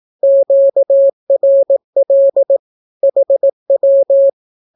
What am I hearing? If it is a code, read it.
Morse code, heard as a keyed tone, about 18 wpm, 550 Hz. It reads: QRL HW
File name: naukapodstaw / skroty / skroty